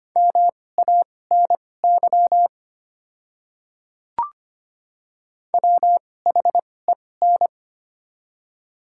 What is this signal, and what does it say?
Morse code, heard as a keyed tone, about 25 wpm, spaced as for 18 wpm, 690 Hz. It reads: MANY WHEN
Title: 2x Character Spacing - Top 200 Words - 25wpm